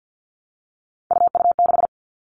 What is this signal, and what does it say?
Morse code, heard as a keyed tone, about 50 wpm, 700 Hz. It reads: UU6
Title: All Cut Numbers - 3-Digit 50wpm